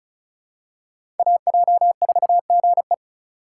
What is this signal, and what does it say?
Morse code, heard as a keyed tone, about 36 wpm, 700 Hz. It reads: AJ4GE
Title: USA Call Signs - 5 Characters - 4x Character Spacing - 35wpm